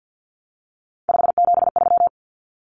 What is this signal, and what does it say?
Morse code, heard as a keyed tone, about 50 wpm, 700 Hz. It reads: 573
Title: Numbers - 3-Digit 50wpm